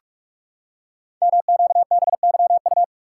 Morse code, heard as a keyed tone, about 44 wpm, 690 Hz. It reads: MQBYU
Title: Code Groups - 5 Letters - 45wpm